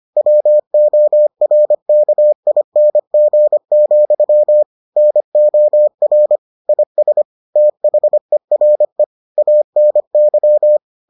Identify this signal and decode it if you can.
Morse code, heard as a keyed tone, about 25 wpm, 600 Hz. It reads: WORKING, NOR IS THERE ANY